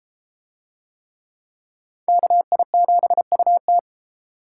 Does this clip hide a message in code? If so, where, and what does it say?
Yes: Morse code from 2.1 to 3.8 s, heard as a keyed tone, about 31 wpm, 700 Hz. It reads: KI7UT